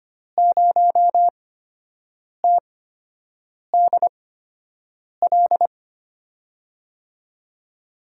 Morse code, heard as a keyed tone, about 25 wpm, spaced as for 7 wpm, 700 Hz. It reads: 0TDL
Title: USA Call Signs - 6 Characters - 8x Character Spacing - 25wpm